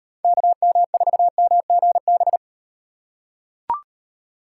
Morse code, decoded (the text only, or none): KM4MGB